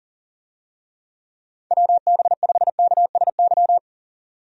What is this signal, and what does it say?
Morse code, heard as a keyed tone, about 40 wpm, 700 Hz. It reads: WB5KSY
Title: USA Call Signs - 6 Characters - 40wpm